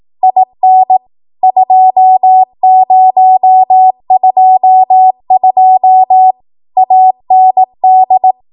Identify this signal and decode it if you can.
Morse code, heard as a keyed tone, about 18 wpm, 750 Hz. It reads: IN 2022 AND